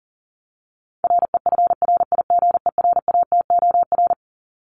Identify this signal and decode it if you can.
Morse code, heard as a keyed tone, about 40 wpm, 700 Hz. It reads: REFRIGERATOR
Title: Top 1501-1750 Words 40wpm